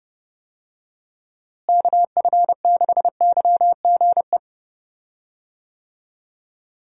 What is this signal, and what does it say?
Morse code, heard as a keyed tone, about 30 wpm, 700 Hz. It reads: KF6YGE
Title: USA Call Signs - 6 Characters - Repeated 2x - 30wpm